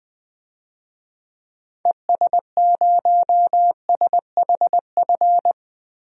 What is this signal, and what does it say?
Morse code, heard as a keyed tone, about 20 wpm, 700 Hz. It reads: ES0SHF
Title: DXCC-100 Call Signs - 6 Characters - Repeated 3x - 20wpm